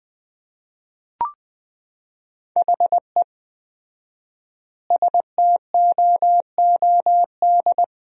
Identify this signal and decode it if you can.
Morse code, heard as a keyed tone, about 20 wpm, 700 Hz. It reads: HE STOOD
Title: 4x Word-Spacing - Sentences from Top 500 Words - 20wpm